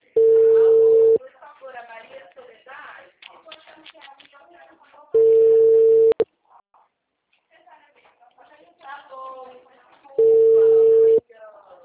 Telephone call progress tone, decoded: ringback tone

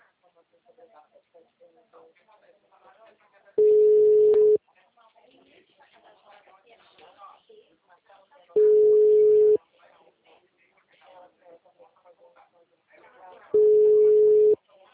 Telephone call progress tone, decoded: ringback tone

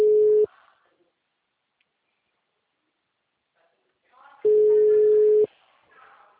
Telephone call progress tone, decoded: ringback tone

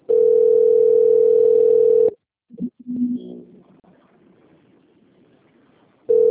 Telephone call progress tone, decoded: ringback tone

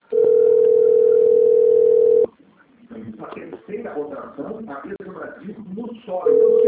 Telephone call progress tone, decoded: ringback tone

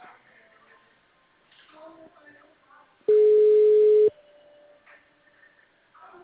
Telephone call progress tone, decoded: ringback tone

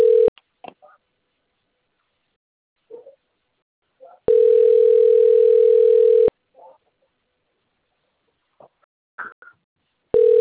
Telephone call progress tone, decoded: ringback tone

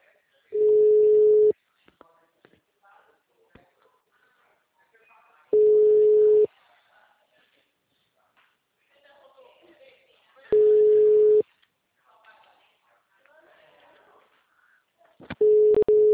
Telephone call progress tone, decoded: ringback tone

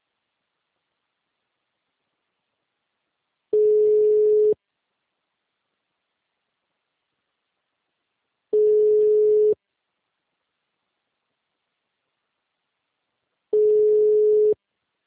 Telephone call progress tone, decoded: ringback tone

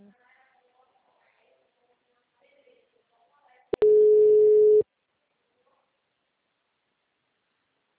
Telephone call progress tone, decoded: ringback tone

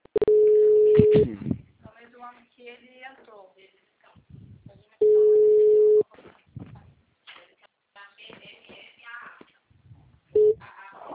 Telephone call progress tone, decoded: ringback tone

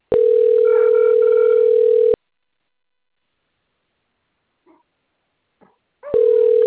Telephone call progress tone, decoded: ringback tone

